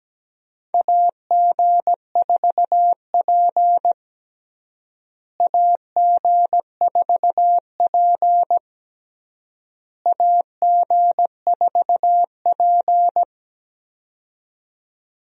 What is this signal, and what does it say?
Morse code, heard as a keyed tone, about 17 wpm, 700 Hz. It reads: AG4P AG4P AG4P
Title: USA Call Signs - 4 Characters - Repeated 3x - 17wpm